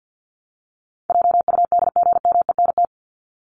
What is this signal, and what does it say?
Morse code, heard as a keyed tone, about 50 wpm, 700 Hz. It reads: JUDGMENT
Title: Top 1751-2000 Words 50wpm